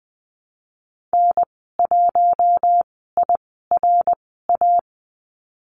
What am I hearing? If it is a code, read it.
Morse code, heard as a keyed tone, about 20 wpm, spaced as for 15 wpm, 700 Hz. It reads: N1IRA
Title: 088 Lesson 15g - Number 5 - Call Signs 20wpm